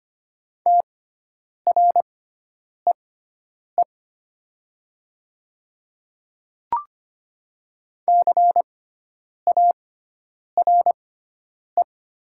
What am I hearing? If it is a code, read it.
Morse code, heard as a keyed tone, about 25 wpm, spaced as for 9 wpm, 700 Hz. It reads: TREE CARE
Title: 6x Character Spacing - Top 500 Words - 25wpm